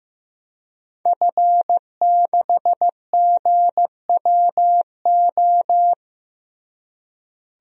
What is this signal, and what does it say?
Morse code, heard as a keyed tone, about 15 wpm, 700 Hz. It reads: F6GWO
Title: DXCC-100 Call Signs - 5 Characters - Repeated 3x - 15wpm